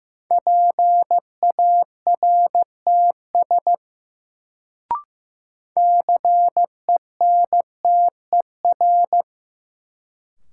Morse code, heard as a keyed tone, about 15 wpm, 700 Hz. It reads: PARTS CENTER
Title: Sets of 3 Words - New Word Drill - 15wpm